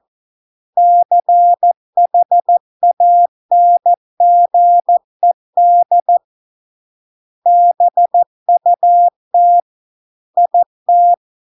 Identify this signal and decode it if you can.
Morse code, heard as a keyed tone, about 14 wpm, 700 Hz. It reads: CHANGED BUT IT